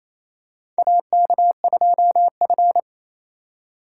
Morse code, heard as a keyed tone, about 28 wpm, 700 Hz. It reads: AK2F
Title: USA Call Signs - 4 Characters - 8x Character Spacing - 28wpm